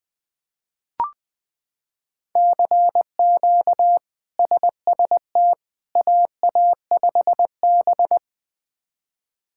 Morse code, heard as a keyed tone, about 20 wpm, 700 Hz. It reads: CQ SST AA5B
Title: K1USN SST QSO Practice - 20wpm